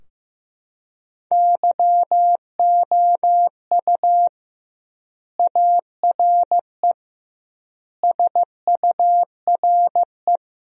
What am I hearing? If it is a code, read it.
Morse code, heard as a keyed tone, about 15 wpm, 700 Hz. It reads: YOU ARE SURE
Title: 2x Word-Spacing - Sentences from Top 300 Words - 15wpm